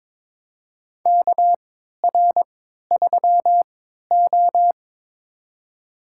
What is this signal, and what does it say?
Morse code, heard as a keyed tone, about 22 wpm, spaced as for 13 wpm, 700 Hz. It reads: KR3O
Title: USA Call Signs - 4 Characters - 3x Character Spacing - 22wpm